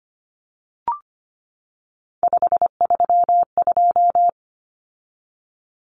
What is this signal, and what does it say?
Morse code, heard as a keyed tone, about 25 wpm, 700 Hz. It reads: 532
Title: Numbers - 3-Digit 25wpm